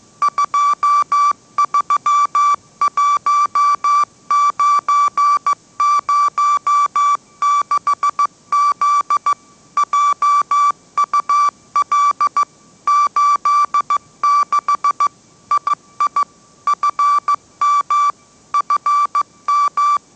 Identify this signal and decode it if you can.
Morse code, heard as a keyed tone, about 15 wpm, 1160 Hz. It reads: 231906Z JUL 86 II FM FM
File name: mrx / mrx066 16wpm